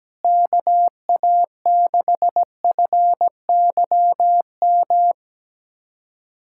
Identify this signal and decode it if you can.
Morse code, heard as a keyed tone, about 17 wpm, 700 Hz. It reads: KA6FYM
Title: USA Callsign DE Callsign - 6 Characters - 17wpm